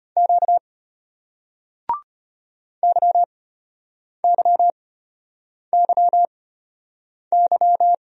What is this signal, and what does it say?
Morse code, heard as a keyed tone, about 31 wpm, spaced as for 8 wpm, 700 Hz. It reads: Q YYYY
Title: Speed-Racing Single Letters 25wpm